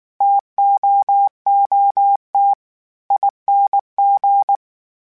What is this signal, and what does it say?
Morse code, heard as a keyed tone, about 19 wpm, 800 Hz. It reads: TOOTING